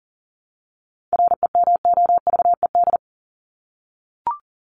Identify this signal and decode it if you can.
Morse code, heard as a keyed tone, about 40 wpm, 700 Hz. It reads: REMOVED
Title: ED Suffix - Top 100 Words with Base Words 40wpm